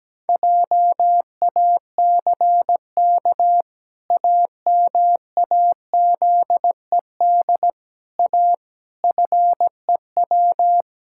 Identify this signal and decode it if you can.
Morse code, heard as a keyed tone, about 17 wpm, 700 Hz. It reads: JACK AMAZED A FEW
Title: Pangram Challenge v2 - 17wpm